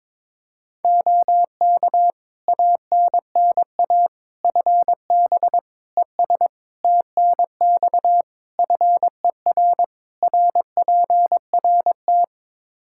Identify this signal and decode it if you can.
Morse code, heard as a keyed tone, about 22 wpm, 700 Hz. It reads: OK ANNA FB ES TNX FER RPRT
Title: Practice QSO - Exchange 2 - Info - 22wpm